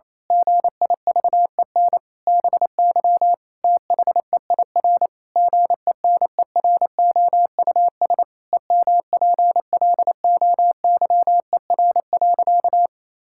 Morse code, heard as a keyed tone, about 28 wpm, 700 Hz. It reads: GIVEN BY THEIR GENEROUS EMPLOYER.
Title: Pangram Challenge v2 - 28wpm